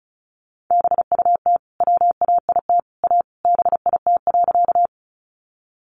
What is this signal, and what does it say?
Morse code, heard as a keyed tone, about 34 wpm, 700 Hz. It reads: BUT WAIT A BIT.
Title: Sentences from Top 900 Words 35wpm